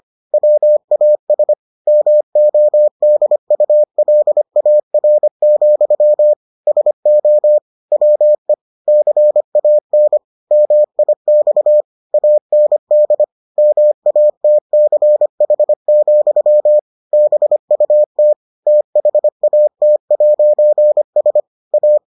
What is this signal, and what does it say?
Morse code, heard as a keyed tone, about 25 wpm, 600 Hz. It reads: WAS MODULAR, SO WE CAN MIX AND MATCH, BUT THAT'S A